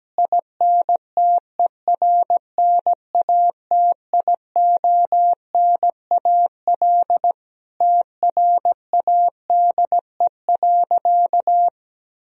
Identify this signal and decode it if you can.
Morse code, heard as a keyed tone, about 17 wpm, 700 Hz. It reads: INTERNATIONAL TRADE.
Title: Sentences from Top 1000 Words - Review Part 4 - 17wpm